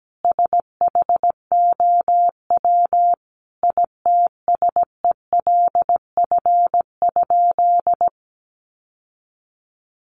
Morse code, heard as a keyed tone, about 17 wpm, 700 Hz. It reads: SHOW ITSELF?
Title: Sentences from Top 800 Words 17wpm